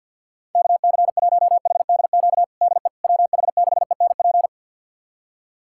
Morse code, heard as a keyed tone, about 50 wpm, 700 Hz. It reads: KC1HDQ DE WH6ENP
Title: USA Callsign DE Callsign - 6 Characters - 50wpm